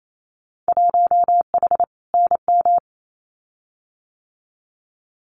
Morse code, heard as a keyed tone, about 28 wpm, 700 Hz. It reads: 1H NM
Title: Winter Field Day Exchanges 28wpm